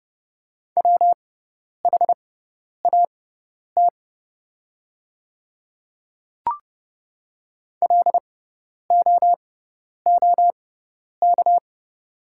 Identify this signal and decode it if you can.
Morse code, heard as a keyed tone, about 30 wpm, spaced as for 10 wpm, 690 Hz. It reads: WHAT LOOK
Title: 6x Character Spacing - Top 100 Words - 30wpm